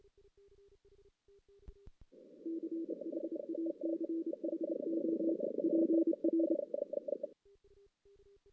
Morse code, heard as a keyed tone, about 28 wpm, 330 Hz. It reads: CHACHO PR
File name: 2016 JAN CW NAQP / 2016 Jan naqp k5ot - 0-5